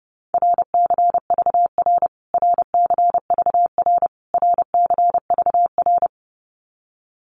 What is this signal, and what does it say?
Morse code, heard as a keyed tone, about 30 wpm, 700 Hz. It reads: RCVR RCVR RCVR
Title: Abbreviations Repeated 3x - 30wpm